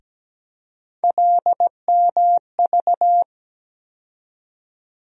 Morse code, heard as a keyed tone, about 17 wpm, 700 Hz. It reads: LMV
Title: USA Call Sign Postfixes - 3 Letters 17wpm